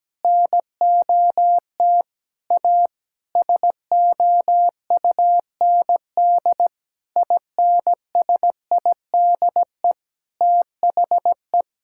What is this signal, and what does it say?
Morse code, heard as a keyed tone, about 17 wpm, 700 Hz. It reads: NOT A SOUND INSIDE THE
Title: Sentences from Top 1000 Words - Review Part 6 - 17wpm